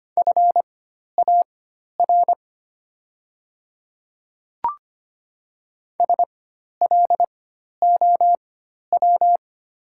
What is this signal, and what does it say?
Morse code, heard as a keyed tone, about 25 wpm, spaced as for 12 wpm, 700 Hz. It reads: FAR SLOW
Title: 4x Character Spacing - Top 500 Words - 25wpm